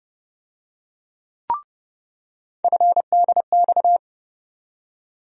Morse code, heard as a keyed tone, about 29 wpm, 700 Hz. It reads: FDX
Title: USA Call Sign Postfixes - 3 Letters 30wpm